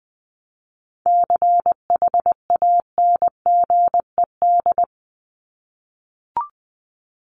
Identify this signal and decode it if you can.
Morse code, heard as a keyed tone, about 20 wpm, 700 Hz. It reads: CHANGED
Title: ED Suffix - Top 100 Words with Base Words 20wpm